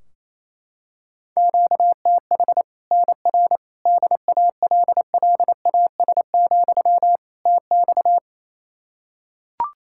Morse code, heard as a keyed tone, about 28 wpm, 700 Hz. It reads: QTH NR DALLAS, TX
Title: Practice QSO - Exchange 1 - Beginning - 28wpm